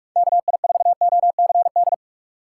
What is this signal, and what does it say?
Morse code, heard as a keyed tone, about 45 wpm, 700 Hz. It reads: KI4OCD
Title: USA Call Signs - 6x Character Spacing - 45wpm